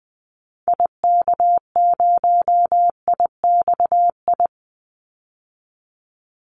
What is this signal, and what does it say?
Morse code, heard as a keyed tone, about 20 wpm, 700 Hz. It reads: IK0IXI
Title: Call Signs Repeated 3x - 20wpm